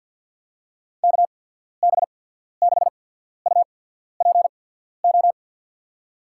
Morse code, heard as a keyed tone, about 48 wpm, spaced as for 14 wpm, 700 Hz. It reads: KB6UPO
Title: USA Call Signs - 6 Characters - 8x Character Spacing - 50wpm